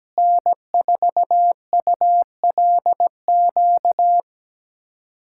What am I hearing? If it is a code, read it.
Morse code, heard as a keyed tone, about 17 wpm, 700 Hz. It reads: N4ULQ
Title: USA Callsign DE Callsign - 5 Characters - 17wpm